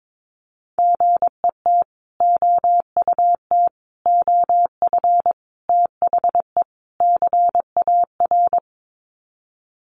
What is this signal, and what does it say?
Morse code, heard as a keyed tone, about 22 wpm, 700 Hz. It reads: GET OUT OF THE CAR
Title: Sets of 5 Words Encore2 22wpm